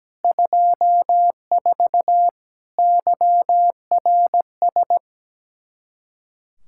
Morse code, heard as a keyed tone, about 17 wpm, 700 Hz. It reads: 24 YRS
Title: Practice QSO - Exchange 3 - More Info - 17wpm